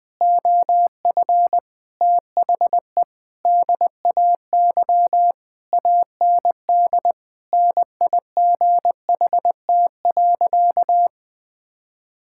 Morse code, heard as a keyed tone, about 21 wpm, 700 Hz. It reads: OF THE DAY AND NIGHT.
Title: Sentences with 8-Word Phrases - Spoken Beginning - 20wpm